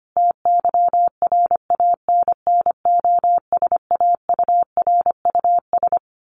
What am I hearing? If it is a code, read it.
Morse code, heard as a keyed tone, about 25 wpm, 700 Hz. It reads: TYRANNOSAURUS
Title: A Sound Of Thunder 25wpm